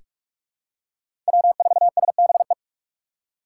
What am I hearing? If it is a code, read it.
Morse code, heard as a keyed tone, about 45 wpm, 700 Hz. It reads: W4SBE
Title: USA Call Signs - 5 Characters - 6x Character Spacing - 45wpm